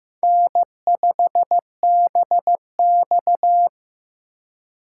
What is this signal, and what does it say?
Morse code, heard as a keyed tone, about 15 wpm, 700 Hz. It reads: N5BX
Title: USA Call Signs - 4 Characters - 3x Character Spacing - 15wpm